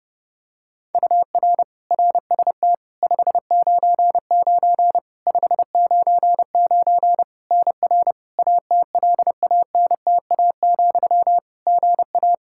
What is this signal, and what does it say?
Morse code, heard as a keyed tone, about 30 wpm, 700 Hz. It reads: UR RST 599 599 NR ATLANTA, GA
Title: Practice QSO - Exchange 1 - Beginning - 30wpm